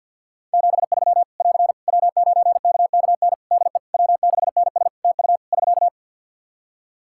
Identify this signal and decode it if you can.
Morse code, heard as a keyed tone, about 48 wpm, 700 Hz. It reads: 73 <AR> W9KKN DE W6NS TU <SK>
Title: Practice QSO - Exchange 4 - Ending - 50wpm